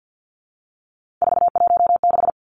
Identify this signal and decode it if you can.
Morse code, heard as a keyed tone, about 48 wpm, 700 Hz. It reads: 416
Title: Numbers - 3-Digit 50wpm